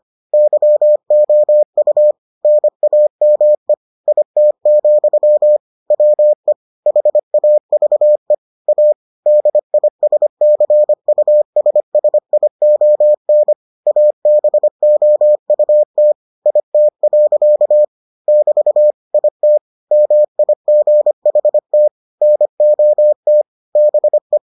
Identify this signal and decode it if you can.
Morse code, heard as a keyed tone, about 25 wpm, 600 Hz. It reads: YOU NAME IT, WE HAVE A DISCUSSION ABOUT IT. <BT> IT MIGHT NOT BE